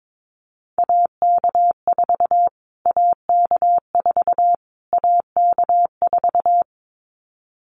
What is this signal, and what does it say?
Morse code, heard as a keyed tone, about 22 wpm, 700 Hz. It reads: AK4 AK4 AK4
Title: Callsign Prefixes for Top 25 DXCC - Repeated 3x - 22wpm